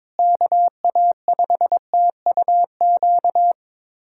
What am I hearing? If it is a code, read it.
Morse code, heard as a keyed tone, about 22 wpm, 700 Hz. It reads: KA5TUQ
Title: USA Callsign DE Callsign - 6 Characters - 22wpm